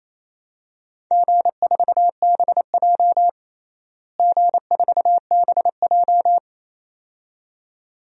Morse code, heard as a keyed tone, about 28 wpm, 700 Hz. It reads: G4BJ G4BJ
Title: DXCC-100 Call Signs - 4 Characters - Repeated 2x - 28wpm